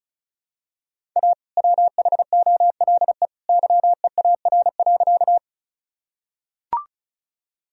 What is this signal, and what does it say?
Morse code, heard as a keyed tone, about 36 wpm, 700 Hz. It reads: A WHOLE YEAR.
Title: Sentences from Top 1000 Words - Review Part 1 - 35wpm